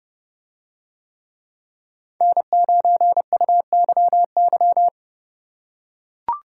Morse code, heard as a keyed tone, about 29 wpm, 700 Hz. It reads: N9UYY